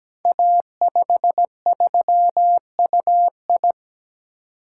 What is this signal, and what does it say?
Morse code, heard as a keyed tone, about 17 wpm, 700 Hz. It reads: A53UI
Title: Code Groups - 5 Letter:Numbers - 17wpm